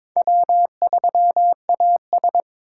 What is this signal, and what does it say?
Morse code, heard as a keyed tone, about 22 wpm, 700 Hz. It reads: W3AS
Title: USA Call Signs - 4 Characters - 4x Character Spacing - 22wpm